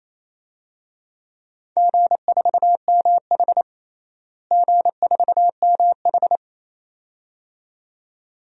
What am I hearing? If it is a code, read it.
Morse code, heard as a keyed tone, about 28 wpm, 700 Hz. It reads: G4MH G4MH
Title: DXCC-100 Call Signs - 4 Characters - Repeated 2x - 28wpm